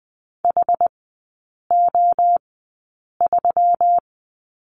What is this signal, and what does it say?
Morse code, heard as a keyed tone, about 19 wpm, spaced as for 8 wpm, 700 Hz. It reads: HO3
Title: 138 Lesson 23e - Letter f - Three Characters 20wpm